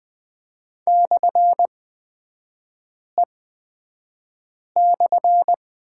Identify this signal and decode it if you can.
Morse code, heard as a keyed tone, about 20 wpm, spaced as for 5 wpm, 700 Hz. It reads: /E/
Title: 224 Lesson 36e - BT - Three Characters 20wpm